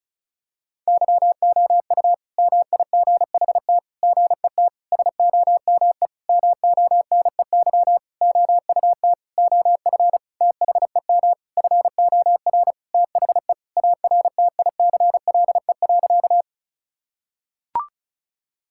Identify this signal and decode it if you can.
Morse code, heard as a keyed tone, about 34 wpm, 700 Hz. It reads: YOU MIGHT GET SOME MONEY OUT OF THEM FOR THE ARTICLE.
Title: Sentences from Top 1000 Words - Review Part 5 - 35wpm